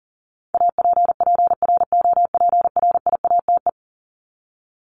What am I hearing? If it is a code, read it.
Morse code, heard as a keyed tone, about 38 wpm, 700 Hz. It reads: APPROPRIATE
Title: Top 751-1000 Words 40wpm